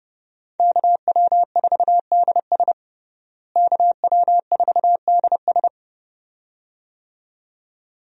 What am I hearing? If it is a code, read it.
Morse code, heard as a keyed tone, about 30 wpm, 700 Hz. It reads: KW4DS KW4DS
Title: USA Call Signs - 5 Characters - Repeated 2x - 30wpm